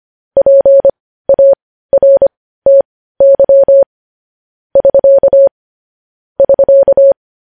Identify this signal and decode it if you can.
Morse code, heard as a keyed tone, about 25 wpm, spaced as for 15 wpm, 550 Hz. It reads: PARTY <SK> <SK>
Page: news-headlines-in-morse-code-at-15-wpm